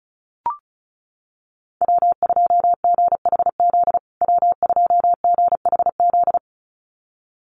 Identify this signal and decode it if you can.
Morse code, heard as a keyed tone, about 34 wpm, 700 Hz. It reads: W2GHZ W2GHZ
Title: USA Call Signs Repeated 2x 35wpm